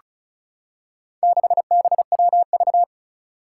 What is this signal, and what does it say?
Morse code, heard as a keyed tone, about 36 wpm, 700 Hz. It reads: 6BWV
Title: Four Challenging Letter-Number Combinations 35wpm